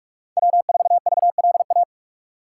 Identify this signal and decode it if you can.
Morse code, heard as a keyed tone, about 46 wpm, 700 Hz. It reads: W4VLA
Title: USA Call Signs - 5 Characters - 3x Character Spacing - 45wpm